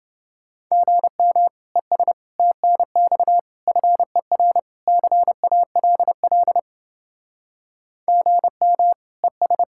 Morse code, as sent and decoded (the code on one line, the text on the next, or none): --. -- / . ... / - -. -..- / ..-. .-. / -.-. .- .-.. .-.. / --. -- / . ...
GM ES TNX FER CALL GM ES